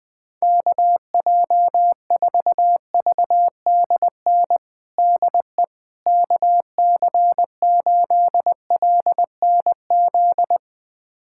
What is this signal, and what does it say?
Morse code, heard as a keyed tone, about 20 wpm, 700 Hz. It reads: KJ4VDN DE KC8LNZ
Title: USA Callsign DE Callsign - 6 Characters - 20wpm